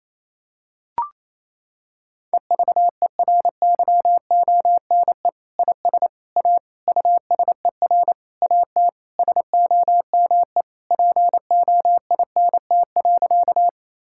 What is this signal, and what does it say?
Morse code, heard as a keyed tone, about 28 wpm, 700 Hz. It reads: EVERYONE IS A USER AT SOME POINT.
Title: Sentences from Top 1000 Words - Review Part 5 - 28wpm